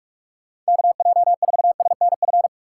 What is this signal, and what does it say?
Morse code, heard as a keyed tone, about 57 wpm, 700 Hz. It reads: KJ4SNF